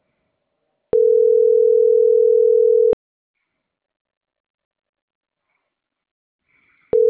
Ringback tone (North America), 440 + 480 Hz.
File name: CENTRAL MAYAB / out lizbeth.mayab